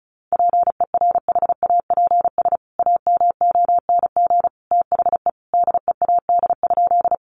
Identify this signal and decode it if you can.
Morse code, heard as a keyed tone, about 34 wpm, 700 Hz. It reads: PERHAPS AMONG THE DEAD?
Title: Sentences from Top 1500 Words 35wpm